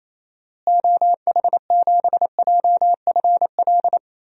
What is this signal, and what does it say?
Morse code, heard as a keyed tone, about 27 wpm, 700 Hz. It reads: OH7JFL
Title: DXCC-100 Call Signs - 6 Characters - Repeated 3x - 28wpm